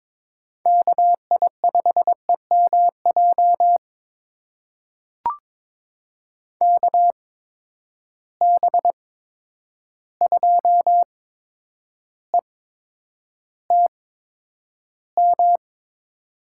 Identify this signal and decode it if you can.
Morse code, heard as a keyed tone, about 22 wpm, spaced as for 9 wpm, 700 Hz. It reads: KI5EMJ KB2ETM
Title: USA Call Signs - 6 Characters - 8x Character Spacing - 22wpm